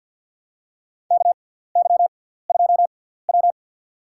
Morse code, heard as a keyed tone, about 52 wpm, spaced as for 17 wpm, 700 Hz. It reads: KY2W
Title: USA Call Signs - 6x Character Spacing - 50wpm